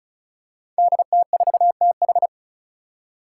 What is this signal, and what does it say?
Morse code, heard as a keyed tone, about 36 wpm, 700 Hz. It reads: DT4TH
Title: Five Challenging Letter-Number Combinations 35wpm